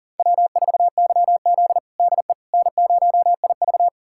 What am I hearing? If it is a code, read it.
Morse code, heard as a keyed tone, about 40 wpm, 700 Hz. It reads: W4YZ DE N0IV
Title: USA Callsign DE Callsign - 4 Characters - 40wpm